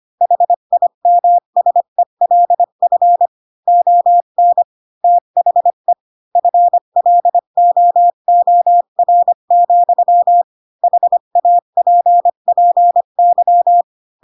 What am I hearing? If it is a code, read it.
Morse code, heard as a keyed tone, about 25 wpm, 700 Hz. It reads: HIMSELF ON THE FLOOR, HAPPY
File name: Pinocchio / Pinocchio ch4 20 25 ik1hge